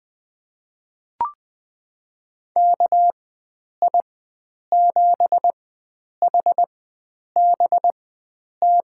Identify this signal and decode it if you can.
Morse code, heard as a keyed tone, about 21 wpm, spaced as for 9 wpm, 700 Hz. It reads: KI7HBT
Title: USA Call Signs - 6 Characters - 4x Character Spacing - 20wpm